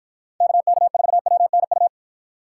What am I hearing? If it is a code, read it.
Morse code, heard as a keyed tone, about 53 wpm, 700 Hz. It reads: KK4WNU